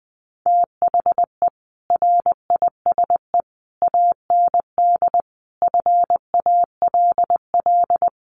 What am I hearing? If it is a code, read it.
Morse code, heard as a keyed tone, about 20 wpm, 700 Hz. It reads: THE RISE AND FALL
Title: Sets of 5 Words Encore2 20wpm